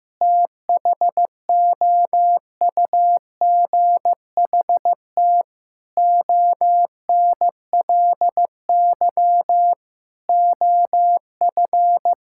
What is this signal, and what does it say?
Morse code, heard as a keyed tone, about 15 wpm, 700 Hz. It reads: THOUGHT ONLY OF
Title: Sentences from Top 1600 Words 15wpm